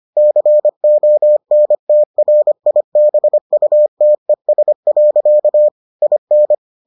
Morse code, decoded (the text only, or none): CONTRIBUTES. IN